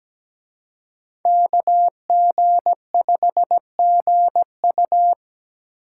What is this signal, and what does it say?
Morse code, heard as a keyed tone, about 20 wpm, 700 Hz. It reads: KG5GU
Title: Speed Racing - USA Call Signs 17wpm